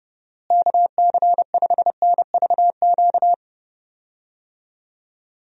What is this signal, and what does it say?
Morse code, heard as a keyed tone, about 32 wpm, 700 Hz. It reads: KC5NVQ
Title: USA Call Signs - 6 Characters - Repeated 2x - 30wpm